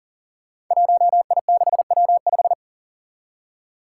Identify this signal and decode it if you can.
Morse code, heard as a keyed tone, about 42 wpm, 700 Hz. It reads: 1I6W5
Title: Five Challenging Letter-Number Combinations 40wpm